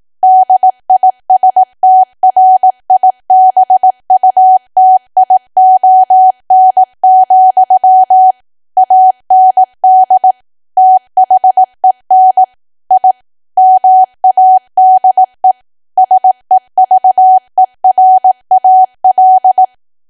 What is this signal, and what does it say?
Morse code, heard as a keyed tone, about 18 wpm, 750 Hz. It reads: DISTRIBUTION, AND THEN I MADE SEVERAL
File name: Archive / 18 WPM / 18WPM